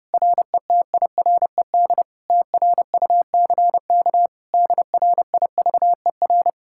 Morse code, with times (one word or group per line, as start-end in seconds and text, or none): 0.1-2.0 RETIRED
2.3-4.3 TRUCK
4.5-6.5 DRIVER